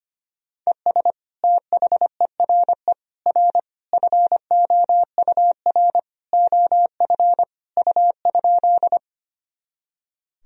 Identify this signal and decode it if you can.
Morse code, heard as a keyed tone, about 25 wpm, 700 Hz. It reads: ES THERE R FOUR OF U?
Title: Sentences from Top 300 Words and Abbreviations 25wpm